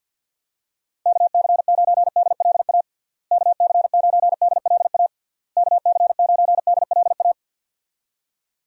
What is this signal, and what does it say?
Morse code, heard as a keyed tone, about 50 wpm, 700 Hz. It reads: KC9DLA KC9DLA KC9DLA
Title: USA Call Signs - 6 Characters - Repeated 3x - 50wpm